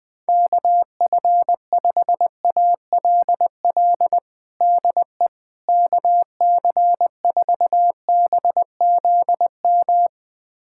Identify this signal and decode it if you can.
Morse code, heard as a keyed tone, about 20 wpm, 700 Hz. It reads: KF5ALL DE KC4BZM
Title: USA Callsign DE Callsign - 6 Characters - 20wpm